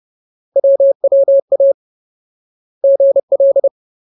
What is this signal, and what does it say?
Morse code, heard as a keyed tone, about 30 wpm, 550 Hz. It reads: WWA GL
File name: Ultimate Word List / 30WPM Ultimate